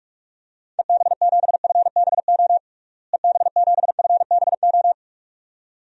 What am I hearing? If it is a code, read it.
Morse code, heard as a keyed tone, about 46 wpm, 700 Hz. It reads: EB7FBO EB7FBO
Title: DXCC-100 Call Signs - 6 Characters - Repeated 2x - 45wpm